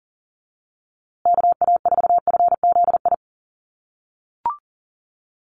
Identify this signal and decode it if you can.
Morse code, heard as a keyed tone, about 40 wpm, 700 Hz. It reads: KA4FZI
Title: USA Call Signs Repeated 2x 40wpm